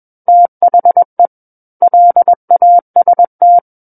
Morse code, heard as a keyed tone, about 21 wpm, 700 Hz. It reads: THE LAST
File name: Book One / 12-Chapter 12 - 21 WPM